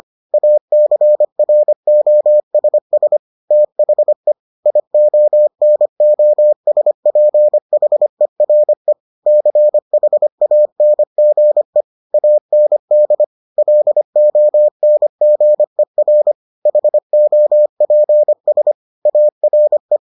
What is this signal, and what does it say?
Morse code, heard as a keyed tone, about 25 wpm, 600 Hz. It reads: ACROSS THE IONOSPHERE CHANGE AND LONGER HOPS ARE